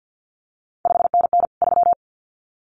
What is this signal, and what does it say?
Morse code, heard as a keyed tone, about 50 wpm, 700 Hz. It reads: 5NN 3
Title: CQ WW DX Contest Exchange - 50wpm